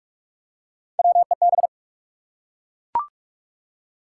Morse code, heard as a keyed tone, about 44 wpm, 700 Hz. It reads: WEB
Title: Three Letter Words - 45wpm